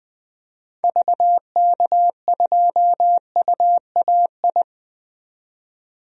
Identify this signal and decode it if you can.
Morse code, heard as a keyed tone, about 20 wpm, 700 Hz. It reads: VK2UAI
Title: DXCC-100 Call Signs - 6 Characters - 20wpm